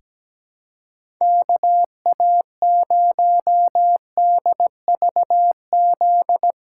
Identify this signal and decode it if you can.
Morse code, heard as a keyed tone, about 17 wpm, 700 Hz. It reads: KA0DVZ
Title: USA Call Signs - 6 Characters - 4x Character Spacing - 17wpm